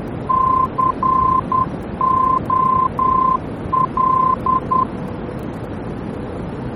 Morse code, heard as a keyed tone, about 10 wpm, 1040 Hz, with noise. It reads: COL
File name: germany / mx / COL-D-VOR